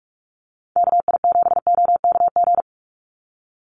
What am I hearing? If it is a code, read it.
Morse code, heard as a keyed tone, about 46 wpm, 700 Hz. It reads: KI7OKG
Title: USA Call Signs 45wpm